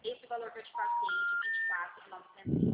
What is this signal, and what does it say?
Special information tone, with speech.